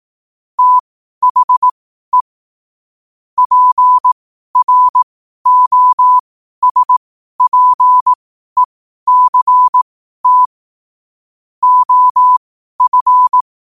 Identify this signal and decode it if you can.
Morse code, heard as a keyed tone, about 18 wpm, spaced as for 12 wpm, 1000 Hz. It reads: THE PROSPECT OF
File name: qotdfiles / 12wpm